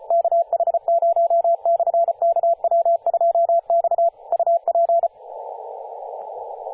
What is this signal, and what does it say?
Morse code, heard as a keyed tone, about 34 wpm, 680 Hz, with radio noise. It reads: KH0/KW2X UP
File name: posts / 2016-sep-22-0537z-kh0-kw2x-18-077mhz-cw